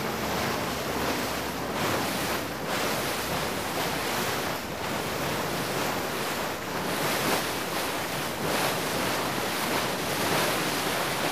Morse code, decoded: none